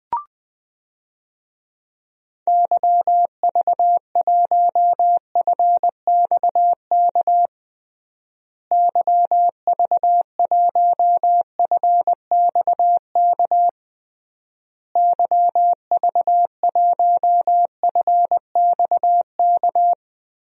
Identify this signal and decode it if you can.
Morse code, heard as a keyed tone, about 20 wpm, 700 Hz. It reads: YV1FXK YV1FXK YV1FXK
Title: DXCC-100 Call Signs - 6 Characters - Repeated 3x - 20wpm